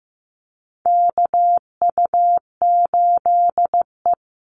Morse code, heard as a keyed tone, about 15 wpm, 700 Hz. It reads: KU8E
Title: Call Signs Repeated 3x - 15wpm